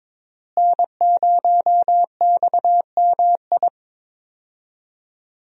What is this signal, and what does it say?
Morse code, heard as a keyed tone, about 21 wpm, 700 Hz. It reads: N0XMI
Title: USA Call Signs - 5 Characters - Repeated 2x - 22wpm